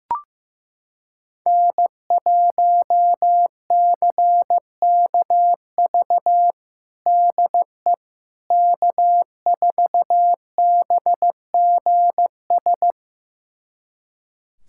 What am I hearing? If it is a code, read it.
Morse code, heard as a keyed tone, about 15 wpm, 700 Hz. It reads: N1CKV DE K4BGS
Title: USA Callsign DE Callsign - 5 Characters - 15wpm